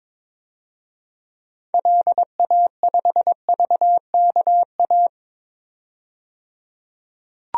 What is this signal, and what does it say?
Morse code, heard as a keyed tone, about 22 wpm, 700 Hz. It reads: LA5VKA
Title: DXCC-100 Call Signs - 6 Characters - 22wpm